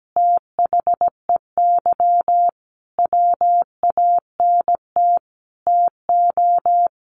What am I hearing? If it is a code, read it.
Morse code, heard as a keyed tone, about 17 wpm, 700 Hz. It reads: THEY WANT TO
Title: Easy Phrases 17wpm